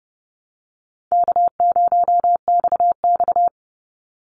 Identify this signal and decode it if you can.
Morse code, heard as a keyed tone, about 30 wpm, 700 Hz. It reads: K0XX
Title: Easy Call Signs Repeated 2x - 30wpm